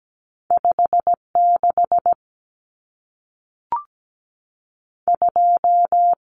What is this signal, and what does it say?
Morse code, heard as a keyed tone, about 17 wpm, 700 Hz. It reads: 56 2
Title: Numbers - 2-Digit 17wpm